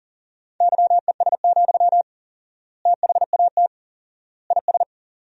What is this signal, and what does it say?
Morse code, heard as a keyed tone, about 40 wpm, 700 Hz. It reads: YES, THAT IS
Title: 4x Word-Spacing - Sentences from Top 500 Words - 40wpm